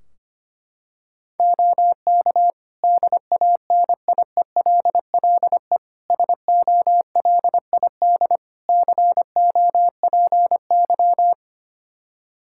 Morse code, heard as a keyed tone, about 25 wpm, 700 Hz. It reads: OK DANIELLE SOLID COPY
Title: Practice QSO - Exchange 3 - More Info - 25wpm